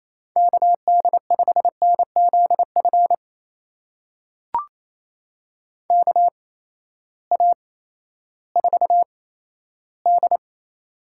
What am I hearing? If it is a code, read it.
Morse code, heard as a keyed tone, about 28 wpm, spaced as for 14 wpm, 700 Hz. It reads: KD5NZF KA4D
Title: USA Call Signs - 8x Character Spacing - 28wpm